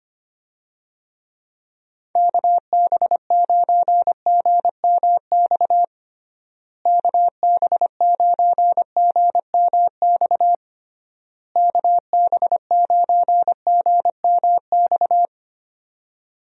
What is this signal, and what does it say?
Morse code, heard as a keyed tone, about 25 wpm, 700 Hz. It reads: KB9GMX KB9GMX KB9GMX
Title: USA Call Signs - 6 Characters - Repeated 3x - 25wpm